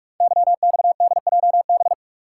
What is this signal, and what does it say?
Morse code, heard as a keyed tone, about 44 wpm, 700 Hz. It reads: YXDJB
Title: Code Groups - 5 Letters - 45wpm